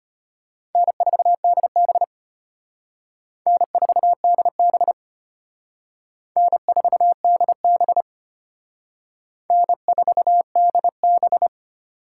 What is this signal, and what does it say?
Morse code, heard as a keyed tone, about 32 wpm, 700 Hz. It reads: N4DB N4DB N4DB N4DB